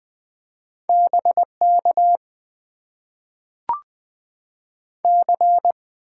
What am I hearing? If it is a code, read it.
Morse code, heard as a keyed tone, about 20 wpm, 700 Hz. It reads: BK C
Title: Top 90 QSO Elements - 6x Character Spacing - 20wpm